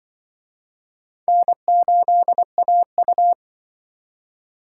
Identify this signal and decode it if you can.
Morse code, heard as a keyed tone, about 24 wpm, 700 Hz. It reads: N8AU
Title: Speed Racing - USA 1x2 Call Signs 20wpm